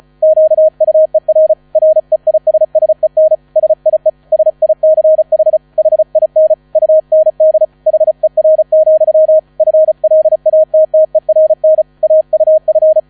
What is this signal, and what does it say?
Morse code, heard as a keyed tone, about 34 wpm, 630 Hz. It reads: QUER REISSEN SIE SICH HIN UND HER, FLATTERN AUF
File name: 35wpm